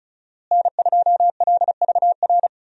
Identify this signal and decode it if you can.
Morse code, heard as a keyed tone, about 34 wpm, 700 Hz. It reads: N2LVR
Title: USA Call Signs - 5 Characters - Repeated 3x - 35wpm